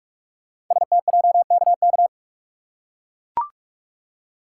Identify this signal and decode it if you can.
Morse code, heard as a keyed tone, about 44 wpm, 690 Hz. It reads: STJKK
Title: Code Groups - 5 Letters - 45wpm